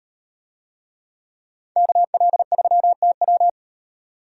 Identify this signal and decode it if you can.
Morse code, heard as a keyed tone, about 38 wpm, 700 Hz. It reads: KL3TW